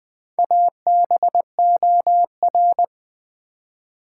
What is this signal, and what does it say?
Morse code, heard as a keyed tone, about 20 wpm, 700 Hz. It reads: ABOR
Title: Build A Word - 2 Letters At A Time - 8-Letter Words - 20wpm